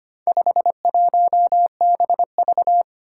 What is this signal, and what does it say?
Morse code, heard as a keyed tone, about 25 wpm, 700 Hz. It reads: 51BV
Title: Four Challenging Letter-Number Combinations 25wpm